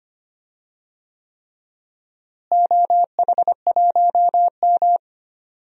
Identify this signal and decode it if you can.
Morse code, heard as a keyed tone, about 25 wpm, 700 Hz. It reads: OH1M
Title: DXCC-100 Call Signs - 4 Characters - Repeated 3x - 25wpm